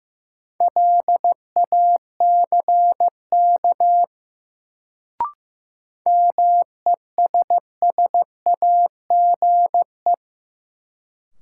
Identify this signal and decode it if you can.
Morse code, heard as a keyed tone, about 15 wpm, 700 Hz. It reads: LACK MESSAGE
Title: Sentences from Top 1100 Words - New Word Drill Alt - 15wpm